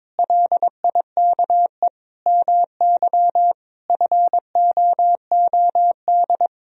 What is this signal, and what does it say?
Morse code, heard as a keyed tone, about 22 wpm, 700 Hz. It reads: LIKE MY FOOD
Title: Sentences from Top 1000 Words - Review Part 3 - 22wpm